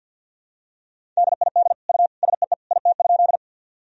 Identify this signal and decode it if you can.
Morse code, heard as a keyed tone, about 53 wpm, 700 Hz. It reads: DID U SEE IT?
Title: Sentences from Top 100 Words and Abbreviations 50wpm